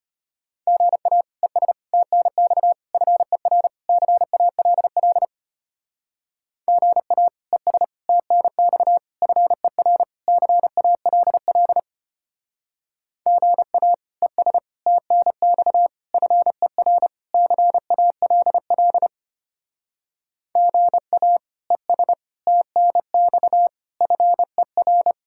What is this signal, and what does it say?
Morse code, heard as a keyed tone, about 31 wpm, 700 Hz. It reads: GA ES TNX FER CALL GA ES TNX FER CALL GA ES TNX FER CALL GA ES TNX FER
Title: Speed-Racing Common QSO Phrases - 25wpm